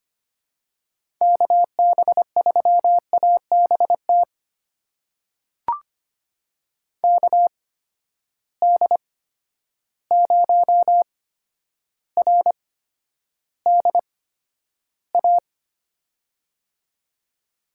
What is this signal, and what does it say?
Morse code, heard as a keyed tone, about 25 wpm, spaced as for 11 wpm, 700 Hz. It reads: KB3ABT KD0RDA
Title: USA Call Signs - 6 Characters - 8x Character Spacing - 25wpm